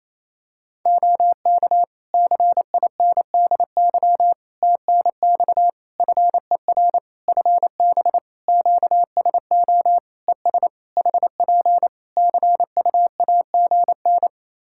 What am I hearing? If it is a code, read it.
Morse code, heard as a keyed tone, about 28 wpm, 700 Hz. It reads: OK CINDY TNX FER FB QSO ES HP CUAGN
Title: K1USN SST QSO Practice 28wpm